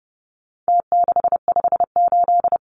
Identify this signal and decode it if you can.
Morse code, heard as a keyed tone, about 30 wpm, 700 Hz. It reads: T658
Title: Numbers - Contest Style - 4-Digit 30wpm